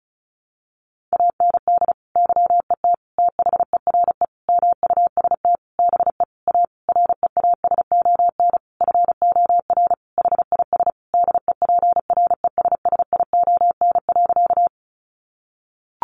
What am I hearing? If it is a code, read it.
Morse code, heard as a keyed tone, about 34 wpm, 700 Hz. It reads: AND YET THERE MUST BE A REASON FOR HIS DEPRESSION.
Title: Sentences from Top 1300 Words 35wpm